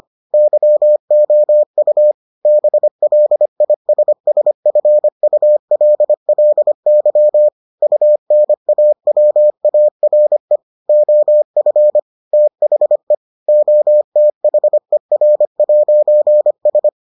Morse code, heard as a keyed tone, about 25 wpm, 600 Hz. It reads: YOU BLISSFULLY UNAWARE OF THE OTHER'S